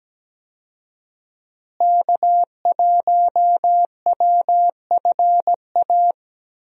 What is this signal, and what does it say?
Morse code, heard as a keyed tone, about 17 wpm, 700 Hz. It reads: K1WFA
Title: USA Call Signs - 5 Characters - Repeated 2x - 17wpm